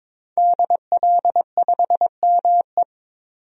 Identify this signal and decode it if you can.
Morse code, heard as a keyed tone, about 22 wpm, 700 Hz. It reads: DL5ME
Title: DXCC-100 Call Signs - 5 Characters - 22wpm